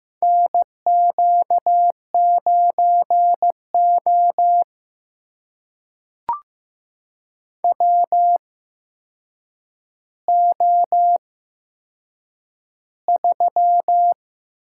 Morse code, heard as a keyed tone, about 15 wpm, spaced as for 7 wpm, 700 Hz. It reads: NQ9O WO3
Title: USA Call Signs - 4 Characters - 8x Character Spacing - 15wpm